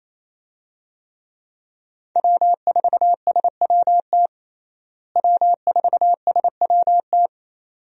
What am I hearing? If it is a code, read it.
Morse code, heard as a keyed tone, about 28 wpm, 700 Hz. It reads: W4SWT W4SWT
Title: USA Call Signs - 5 Characters - Repeated 2x - 28wpm